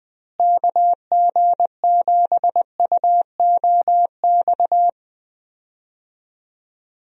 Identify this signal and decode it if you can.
Morse code, heard as a keyed tone, about 20 wpm, 700 Hz. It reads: KG7UOX
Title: USA Call Signs - 6 Characters - Repeated 2x - 20wpm